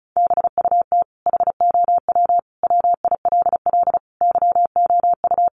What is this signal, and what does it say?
Morse code, heard as a keyed tone, about 34 wpm, 700 Hz. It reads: BUT HOW WILL YOU
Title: Sentences from Top 1400 Words 35wpm